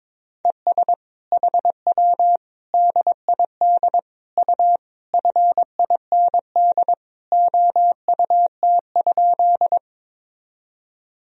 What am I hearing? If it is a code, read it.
Morse code, heard as a keyed tone, about 22 wpm, 700 Hz. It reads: ES HW DID U FIND OUT?
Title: Sentences from Top 200 Words and Abbreviations 22wpm